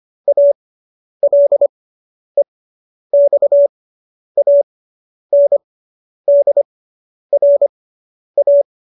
Morse code, heard as a keyed tone, about 25 wpm, spaced as for 10 wpm, 570 Hz. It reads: ALEXANDRA